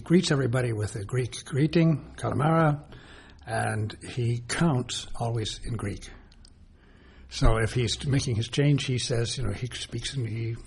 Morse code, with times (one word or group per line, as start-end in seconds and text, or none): none